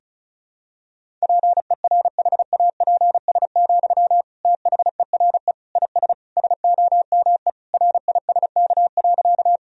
Morse code, heard as a keyed tone, about 36 wpm, 700 Hz. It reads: PERHAPS, THERE IS SOME RISK.
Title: Sentences from Top 1000 Words - Review Part 4 - 35wpm